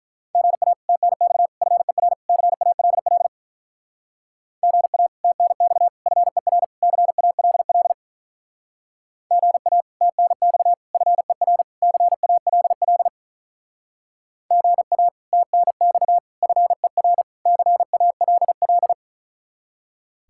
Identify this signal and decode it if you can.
Morse code, heard as a keyed tone, about 46 wpm, 700 Hz. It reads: GA TNX FER CALL GA TNX FER CALL GA TNX FER CALL GA TNX FER CALL